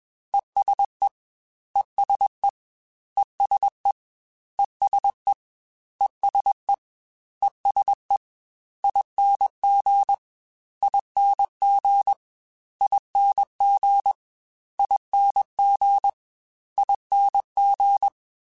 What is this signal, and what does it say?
Morse code, heard as a keyed tone, about 21 wpm, 800 Hz. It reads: ESE ESE ESE ESE ESE ESE ING ING ING ING ING
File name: Special INGs-mp3s / Special INGs-mp3s / ESE ING @20